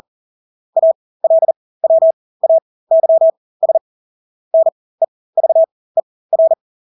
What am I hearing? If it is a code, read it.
Morse code, heard as a keyed tone, about 40 wpm, spaced as for 20 wpm, 650 Hz. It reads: ALWAYS NEVER
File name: Engelske ord 40-20 wpm